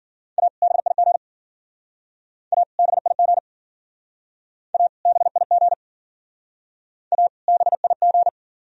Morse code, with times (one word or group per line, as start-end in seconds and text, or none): 0.4-0.5 A
0.6-1.2 BIG
2.5-2.6 A
2.8-3.4 BIG
4.7-4.9 A
5.1-5.7 BIG
7.1-7.3 A
7.5-8.3 BIG